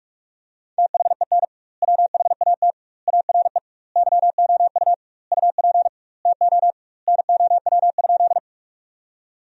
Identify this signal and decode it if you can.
Morse code, heard as a keyed tone, about 46 wpm, 700 Hz. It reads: THEN WHAT ARE YOU UP TO NOW?
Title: 2x Word-Spacing - Sentences from Top 100 Words - 45wpm